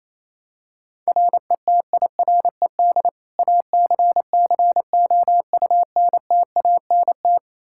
Morse code, heard as a keyed tone, about 27 wpm, 700 Hz. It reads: RETIRED ACCOUNTANT
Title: Practice QSO - Exchange 3 - More Info - 28wpm